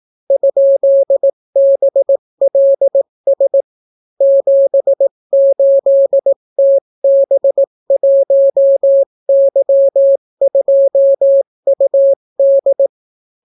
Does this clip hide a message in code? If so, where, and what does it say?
Yes: Morse code from 0.3 to 12.9 s, heard as a keyed tone, about 18 wpm, 550 Hz. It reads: ?BLS 78TB1Y2UD